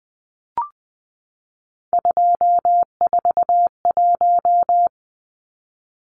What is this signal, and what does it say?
Morse code, heard as a keyed tone, about 20 wpm, 700 Hz. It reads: 241
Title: Numbers - 3-Digit 20wpm